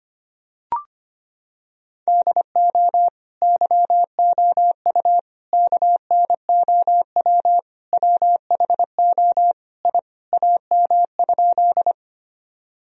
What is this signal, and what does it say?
Morse code, heard as a keyed tone, about 25 wpm, 700 Hz. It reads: DO YOU KNOW WHO I AM?
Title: Sentences from Top 1000 Words - Review Part 2 - 25wpm